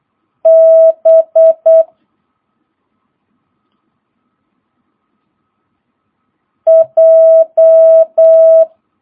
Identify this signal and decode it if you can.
Morse code, heard as a keyed tone, about 8 wpm, 650 Hz. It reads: B J
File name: jh8cbh / 2022cw